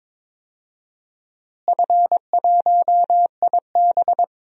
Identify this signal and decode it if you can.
Morse code, heard as a keyed tone, about 22 wpm, 700 Hz. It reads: F1IB
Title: DXCC-100 Call Signs - 4 Characters - Repeated 3x - 22wpm